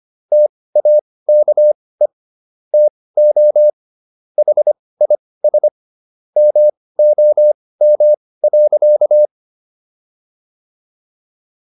Morse code, heard as a keyed tone, about 25 wpm, spaced as for 18 wpm, 600 Hz. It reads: TAKE TO HIS MOM.